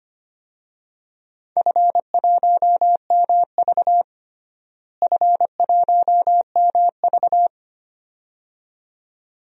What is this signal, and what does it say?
Morse code, heard as a keyed tone, about 25 wpm, 700 Hz. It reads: F1MV F1MV
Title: DXCC-100 Call Signs - 4 Characters - Repeated 2x - 25wpm